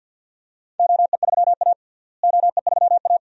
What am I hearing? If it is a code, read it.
Morse code, heard as a keyed tone, about 46 wpm, 700 Hz. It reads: OE3A OE3A
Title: DXCC-100 Call Signs - 4 Characters - Repeated 3x - 50wpm